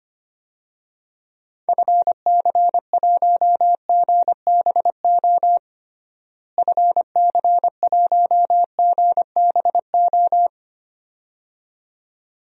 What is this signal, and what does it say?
Morse code, heard as a keyed tone, about 25 wpm, 700 Hz. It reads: FC1GBO FC1GBO
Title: DXCC-100 Call Signs - 6 Characters - Repeated 2x - 25wpm